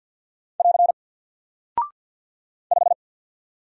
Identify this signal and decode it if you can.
Morse code, heard as a keyed tone, about 48 wpm, 700 Hz. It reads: <AR> 5
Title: Single Character (Letter:Number:Punctuation:Prosign) Alt 50wpm